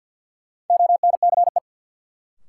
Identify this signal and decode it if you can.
Morse code, heard as a keyed tone, about 53 wpm, 700 Hz. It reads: ONCE
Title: Sets of 4 Words - New Word Drill - 50wpm